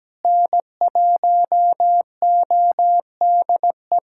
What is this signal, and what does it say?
Morse code, heard as a keyed tone, about 17 wpm, 700 Hz. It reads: N1ODE
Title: USA Callsign DE Callsign - 5 Characters - 17wpm